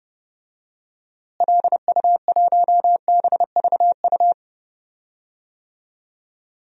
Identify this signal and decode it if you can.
Morse code, heard as a keyed tone, about 32 wpm, 700 Hz. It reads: LU1BVU
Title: DXCC-100 Call Signs - 6 Characters - Repeated 3x - 30wpm